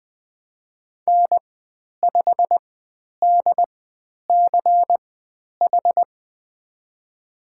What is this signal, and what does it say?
Morse code, heard as a keyed tone, about 20 wpm, spaced as for 10 wpm, 700 Hz. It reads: N5DCH
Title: 102 Lesson 17g - Letter u - Call Signs 20wpm